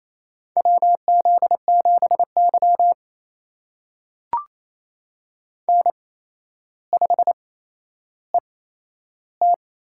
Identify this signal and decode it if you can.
Morse code, heard as a keyed tone, about 28 wpm, spaced as for 12 wpm, 700 Hz. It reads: WZ7Y N5ET
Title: USA Call Signs - 4 Characters - 8x Character Spacing - 28wpm